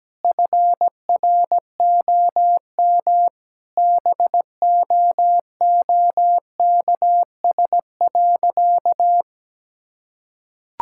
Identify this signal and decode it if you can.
Morse code, heard as a keyed tone, about 17 wpm, 700 Hz. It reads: FROM BOOKS.
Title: Sentences from Top 1000 Words - Review Part 6 - 17wpm